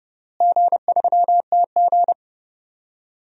Morse code, heard as a keyed tone, about 32 wpm, 700 Hz. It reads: G3TG
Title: DXCC-100 Call Signs - 4 Characters - Repeated 2x - 30wpm